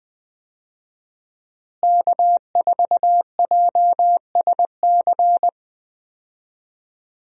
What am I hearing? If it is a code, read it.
Morse code, heard as a keyed tone, about 20 wpm, 700 Hz. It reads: K4JSC
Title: USA Call Signs - 5 Characters - Repeated 3x - 20wpm